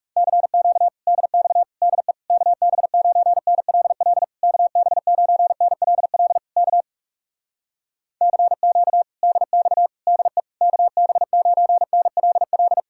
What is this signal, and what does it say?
Morse code, heard as a keyed tone, about 38 wpm, 700 Hz. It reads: CQ DX DE KB9NLL KB9NLL K CQ DX DE KB9NLL